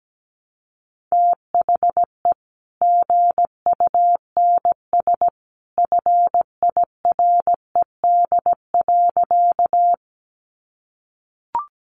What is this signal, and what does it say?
Morse code, heard as a keyed tone, about 17 wpm, 700 Hz. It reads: THE GUNS FIRED.
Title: A Sound Of Thunder 17wpm